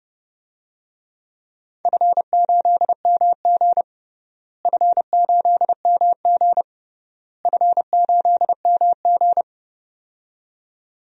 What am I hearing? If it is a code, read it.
Morse code, heard as a keyed tone, about 30 wpm, 700 Hz. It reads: F8MG F8MG F8MG
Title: DXCC-100 Call Signs - 4 Characters - Repeated 3x - 30wpm